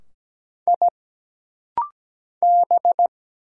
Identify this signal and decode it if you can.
Morse code, heard as a keyed tone, about 17 wpm, 700 Hz. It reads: I B
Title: Single Letters - Rapid-Fire 17wpm